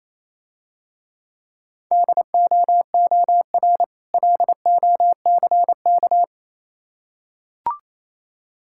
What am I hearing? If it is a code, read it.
Morse code, heard as a keyed tone, about 27 wpm, 700 Hz. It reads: DOOR LOCK
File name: Easy Words - Speed-Racing with Hints - 22wpm